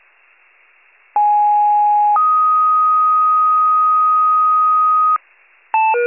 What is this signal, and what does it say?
Two-tone page, heard as a one-second tone, then a three-second tone, about 830 then 1250 Hz. It is heard from 1.2 to 5.2 s.